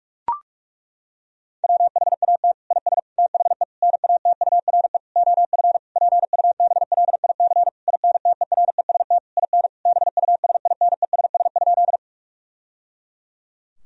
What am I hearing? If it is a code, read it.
Morse code, heard as a keyed tone, about 42 wpm, 700 Hz. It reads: WHAT IS THE NATURE OF PUBLIC INTEREST IN BUSINESS?
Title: Sentences from Top 1000 Words - Review Part 3 - 45wpm